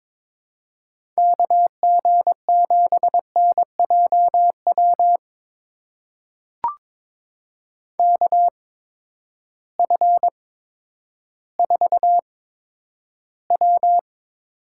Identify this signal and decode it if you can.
Morse code, heard as a keyed tone, about 22 wpm, spaced as for 11 wpm, 700 Hz. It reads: KG7NJW KF4W
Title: USA Call Signs - 6 Characters - 8x Character Spacing - 22wpm